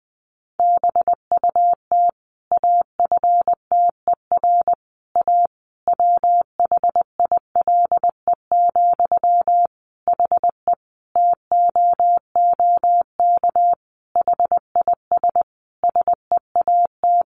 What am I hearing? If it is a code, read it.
Morse code, heard as a keyed tone, about 20 wpm, 700 Hz. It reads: BUT AFTER A WHILE, HE TOOK HIS SEAT
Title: Sentences from Top 1500 Words 20wpm